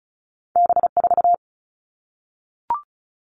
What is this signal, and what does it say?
Morse code, heard as a keyed tone, about 36 wpm, 700 Hz. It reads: B4
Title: All Cut Numbers - 2-Digit 35wpm